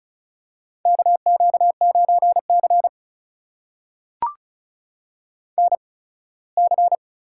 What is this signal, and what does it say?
Morse code, heard as a keyed tone, about 36 wpm, spaced as for 21 wpm, 700 Hz. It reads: KQ9C NC
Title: USA Call Signs - 4 Characters - 8x Character Spacing - 35wpm